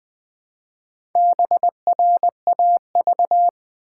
Morse code, heard as a keyed tone, about 20 wpm, 700 Hz. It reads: BRAV
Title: Build A Word - 1 Letter At A Time - 3 to 5-Letter Words - 20wpm